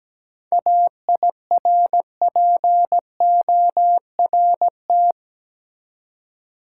Morse code, heard as a keyed tone, about 17 wpm, 700 Hz. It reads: AIRPORT
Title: Seven Letter Words - 17wpm